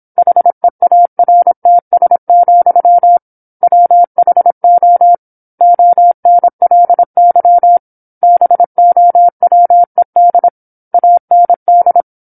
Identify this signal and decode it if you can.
Morse code, heard as a keyed tone, about 27 wpm, 700 Hz. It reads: HEARTS, WHO ONLY BOWED AND
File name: 08-Chapter-08 26 WPM